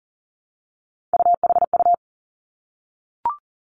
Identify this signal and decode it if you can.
Morse code, heard as a keyed tone, about 38 wpm, 700 Hz. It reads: UHU
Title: Three Challenging Letter-Number Combinations 40wpm